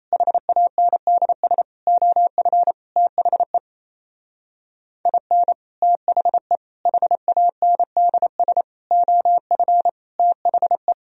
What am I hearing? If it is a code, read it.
Morse code, heard as a keyed tone, about 34 wpm, 700 Hz. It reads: HANDS OF THE IN THE HANDS OF THE
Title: Speed-Racing - Sets of 5 Words 28wpm